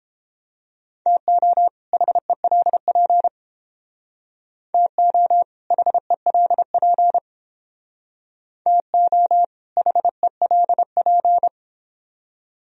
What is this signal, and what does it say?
Morse code, heard as a keyed tone, about 31 wpm, 700 Hz. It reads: TO HELP TO HELP TO HELP